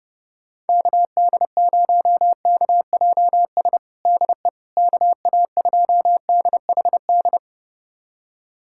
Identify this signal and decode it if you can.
Morse code, heard as a keyed tone, about 31 wpm, 700 Hz. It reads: KD0KJS DE KA2DHD
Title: USA Callsign DE Callsign - 6 Characters - 30wpm